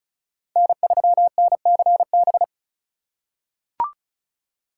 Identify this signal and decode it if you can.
Morse code, heard as a keyed tone, about 36 wpm, 700 Hz. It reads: N3NCB
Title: USA Call Signs - 5 Characters - 2x Character Spacing - 35wpm